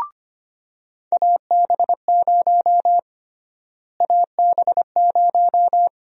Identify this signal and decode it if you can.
Morse code, heard as a keyed tone, about 25 wpm, 700 Hz. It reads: AB0 AB0
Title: USA Call Sign Prefixes - Repeated 2x - 25wpm